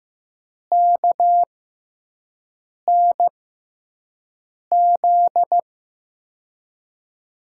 Morse code, heard as a keyed tone, about 15 wpm, spaced as for 5 wpm, 700 Hz. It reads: KNZ
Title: USA Call Signs - 6 Characters - 6x Character Spacing - 15wpm